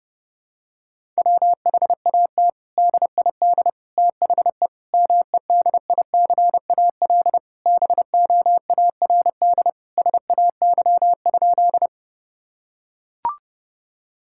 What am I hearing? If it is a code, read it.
Morse code, heard as a keyed tone, about 32 wpm, 700 Hz. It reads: WHAT DID THE MEDICAL BOARD SAY?
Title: Sentences from Top 1000 Words - Review Part 5 - 30wpm